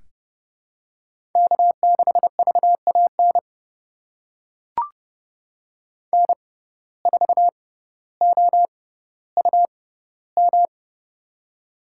Morse code, heard as a keyed tone, about 32 wpm, spaced as for 16 wpm, 700 Hz. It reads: K6VAN N4OUM
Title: USA Call Signs - 5 Characters - 6x Character Spacing - 30wpm